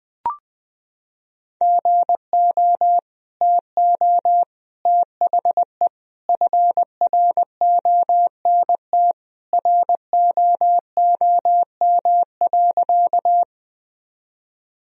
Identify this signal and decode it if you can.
Morse code, heard as a keyed tone, about 21 wpm, 700 Hz. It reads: GO TO THE FRONT ROOM.
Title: Sentences from Top 1000 Words - Review Part 2 - 20wpm